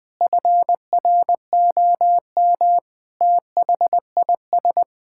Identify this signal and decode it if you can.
Morse code, heard as a keyed tone, about 20 wpm, 700 Hz. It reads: FROM THIS
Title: Sentences from Top 1600 Words 20wpm